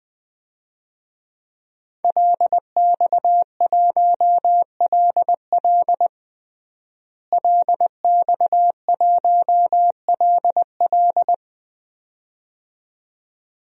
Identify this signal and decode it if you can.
Morse code, heard as a keyed tone, about 20 wpm, 700 Hz. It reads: LX1LL LX1LL
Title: DXCC-100 Call Signs - 5 Characters - Repeated 2x - 20wpm